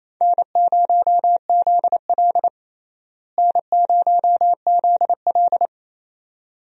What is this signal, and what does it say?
Morse code, heard as a keyed tone, about 27 wpm, 700 Hz. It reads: N0ZL N0ZL
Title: USA Call Signs - 4 Characters - Repeated 2x - 28wpm